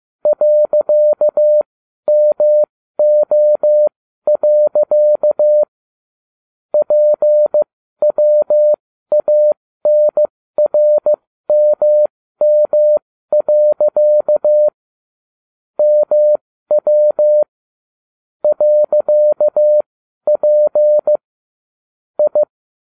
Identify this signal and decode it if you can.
Morse code, heard as a keyed tone, about 15 wpm, spaced as for 12 wpm, 600 Hz. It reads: .MO. PWANRMM. MW .P I